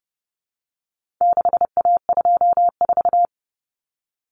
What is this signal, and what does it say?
Morse code, heard as a keyed tone, about 30 wpm, 700 Hz. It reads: 6A24
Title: Common Cut Numbers - 4-Digit 30wpm